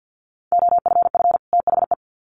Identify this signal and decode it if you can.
Morse code, heard as a keyed tone, about 50 wpm, 700 Hz. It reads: OFF THE
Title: Sets of 2 Words 50wpm